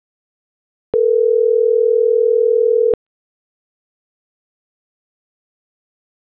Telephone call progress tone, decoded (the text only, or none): ringback tone